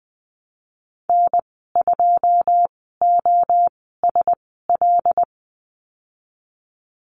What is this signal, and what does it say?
Morse code, heard as a keyed tone, about 20 wpm, spaced as for 15 wpm, 700 Hz. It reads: N2OSL
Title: 154 Lesson 25g - Letter p - Call Signs 20wpm